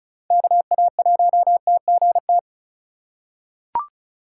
Morse code, heard as a keyed tone, about 36 wpm, 700 Hz. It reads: KA1TGT
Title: USA Call Signs - 6 Characters - 4x Character Spacing - 35wpm